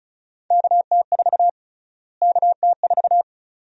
Morse code, heard as a keyed tone, about 34 wpm, 700 Hz. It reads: KT4 KT4
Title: USA Call Sign Prefixes - Repeated 2x - 35wpm